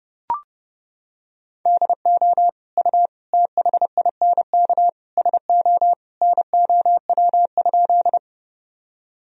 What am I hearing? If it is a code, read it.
Morse code, heard as a keyed tone, about 30 wpm, 700 Hz. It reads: DO U THINK SO NOW?
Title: Sentences from Top 200 Words and Abbreviations 30wpm